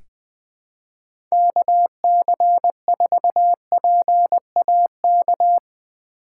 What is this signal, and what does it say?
Morse code, heard as a keyed tone, about 20 wpm, 700 Hz. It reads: KC4PAK
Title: USA Call Signs - 6 Characters - 2x Character Spacing - 20wpm